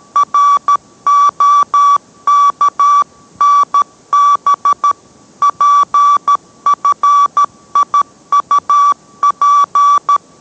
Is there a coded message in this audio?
Yes: Morse code from 0.2 to 10.2 s, heard as a keyed tone, about 13 wpm, 1150 Hz. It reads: ROKNB PFIUP